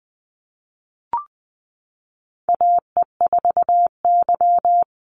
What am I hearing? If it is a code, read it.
Morse code, heard as a keyed tone, about 20 wpm, 700 Hz. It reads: AE4Y
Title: NA SOTA Chaser Callsigns 20wpm